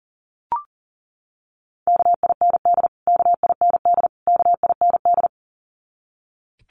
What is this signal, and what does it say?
Morse code, heard as a keyed tone, about 40 wpm, 700 Hz. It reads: KIND KIND KIND
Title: Top 200 Words Repeated 40wpm